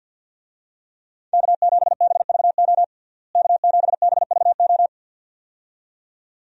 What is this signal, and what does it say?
Morse code, heard as a keyed tone, about 52 wpm, 700 Hz. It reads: K7BVO K7BVO
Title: USA Call Signs - 5 Characters - Repeated 2x - 50wpm